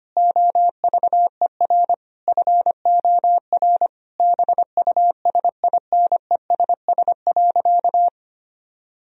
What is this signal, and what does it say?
Morse code, heard as a keyed tone, about 25 wpm, 700 Hz. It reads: OVER FOR BUSINESS.
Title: Sentences from Top 1000 Words - Review Part 3 - 25wpm